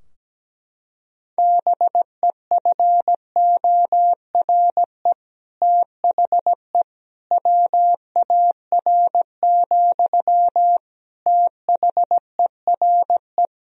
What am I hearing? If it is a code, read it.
Morse code, heard as a keyed tone, about 17 wpm, 700 Hz. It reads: BEFORE THE WAR, THERE
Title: Sentences from Top 1000 Words - Review Part 2 - 17wpm